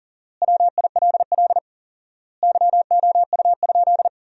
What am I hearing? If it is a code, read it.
Morse code, heard as a keyed tone, about 40 wpm, 700 Hz. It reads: WILL YOU?
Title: 4x Word-Spacing - Sentences from Top 300 Words - 40wpm